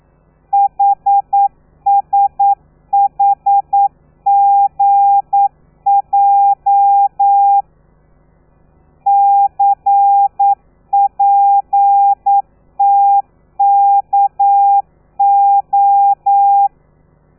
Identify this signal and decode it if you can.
Morse code, heard as a keyed tone, about 9 wpm, 800 Hz. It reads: HSHGJ CPTKO
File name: telegrafia / i2tfj / Lez13 40